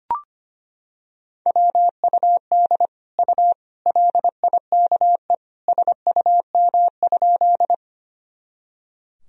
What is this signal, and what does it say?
Morse code, heard as a keyed tone, about 25 wpm, 700 Hz. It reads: WUD U LIKE SUM?
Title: Sentences from Top 100 Words and Abbreviations 25wpm